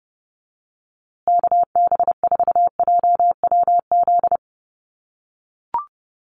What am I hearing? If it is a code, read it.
Morse code, heard as a keyed tone, about 30 wpm, 700 Hz. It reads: KB4JWZ
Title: USA Call Signs Repeated 2x 30wpm